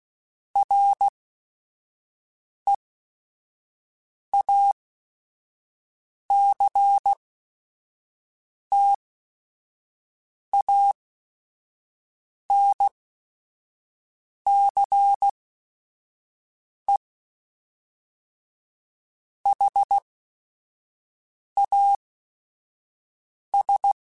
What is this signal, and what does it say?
Morse code, heard as a keyed tone, about 16 wpm, spaced as for 5 wpm, 800 Hz. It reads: REACTANCE HAS
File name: Technical Subjects / Technical Subjects-mp3s / XL @5